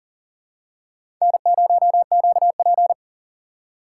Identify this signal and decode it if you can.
Morse code, heard as a keyed tone, about 38 wpm, 700 Hz. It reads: N0QP
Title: USA Call Signs - 4 Characters - 4x Character Spacing - 40wpm